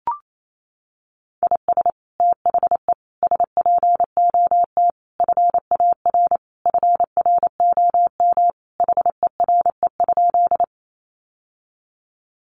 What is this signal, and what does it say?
Morse code, heard as a keyed tone, about 28 wpm, 700 Hz. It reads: IS THE SPOT FAR FROM HERE?
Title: Sentences from Top 1100 Words 28wpm